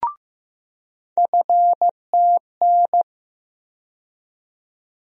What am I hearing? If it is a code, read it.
Morse code, heard as a keyed tone, about 15 wpm, 700 Hz. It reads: FTN
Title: USA Call Sign Postfixes - 3 Letters 15wpm